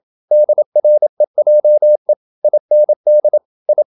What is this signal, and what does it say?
Morse code, heard as a keyed tone, about 27 wpm, 600 Hz. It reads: DREJE IND I
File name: Ferie QRQ 27wpm